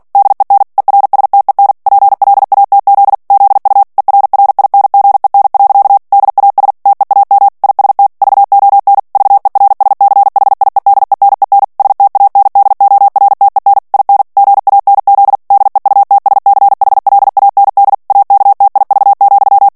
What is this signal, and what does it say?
Morse code, heard as a keyed tone, about 48 wpm, 800 Hz. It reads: DEN ERSTEN PLATZ ZU ERRINGEN. DAS TEAM IST VON VERSCHIEDENEN STANDORTEN IN GANZ DEUTSCHLAND AKTIV,
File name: morse mp3 / ALLEMAND texte clair 2 vitesse 45 mots minute